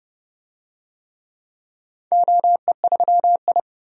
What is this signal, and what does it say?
Morse code, heard as a keyed tone, about 30 wpm, 700 Hz. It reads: OE3I
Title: DXCC-100 Call Signs - 4 Characters - 30wpm